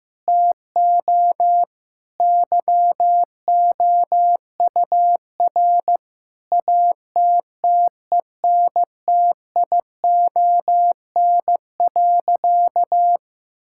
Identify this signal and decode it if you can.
Morse code, heard as a keyed tone, about 16 wpm, 700 Hz. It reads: TO YOUR ATTENTION.
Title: Sentences from Top 1000 Words - Review Part 6 - 15wpm